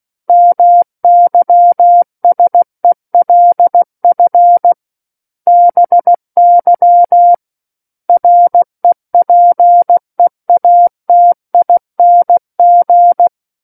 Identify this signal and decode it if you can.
Morse code, heard as a keyed tone, about 16 wpm, 700 Hz. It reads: MYSELF BY REPEATING
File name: Book One / 07-Chapter 7 - 16 WPM